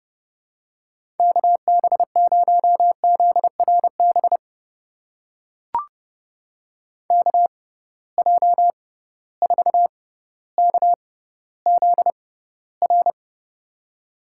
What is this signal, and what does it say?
Morse code, heard as a keyed tone, about 30 wpm, spaced as for 15 wpm, 700 Hz. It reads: KB0ZRB KJ4KZR
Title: USA Call Signs - 6 Characters - 6x Character Spacing - 30wpm